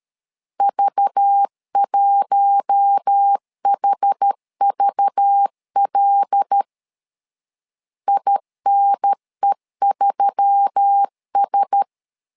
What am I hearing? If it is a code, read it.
Morse code, heard as a keyed tone, about 13 wpm, 800 Hz. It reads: V1HVL INE3S
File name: LEZIONI / 5 LBVC / 5 LBV 12wpm 1855